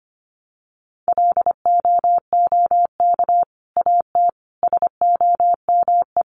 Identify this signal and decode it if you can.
Morse code, heard as a keyed tone, about 25 wpm, 700 Hz. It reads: LOOK AT SOME
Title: Sets of 5 Words Encore 25wpm